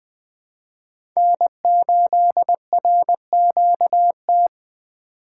Morse code, heard as a keyed tone, about 20 wpm, 700 Hz. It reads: N8RQT
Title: USA Call Signs - 5 Characters - 2x Character Spacing - 20wpm